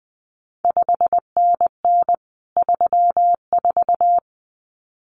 Morse code, heard as a keyed tone, about 19 wpm, 700 Hz. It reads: 5NN 34
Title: CQ WW DX Contest Exchange Encore - 20wpm